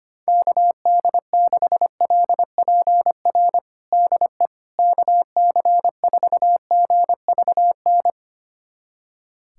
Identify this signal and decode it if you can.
Morse code, heard as a keyed tone, about 25 wpm, 700 Hz. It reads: KD6LPR DE KC4GVN
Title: USA Callsign DE Callsign - 6 Characters - 25wpm